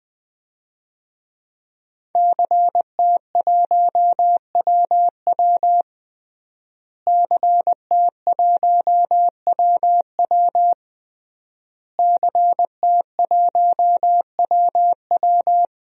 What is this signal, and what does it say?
Morse code, heard as a keyed tone, about 20 wpm, 700 Hz. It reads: CT1WW CT1WW CT1WW
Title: DXCC-100 Call Signs - 5 Characters - Repeated 3x - 20wpm